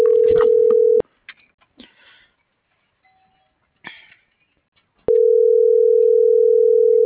Ringback tone (North America), 440 + 480 Hz.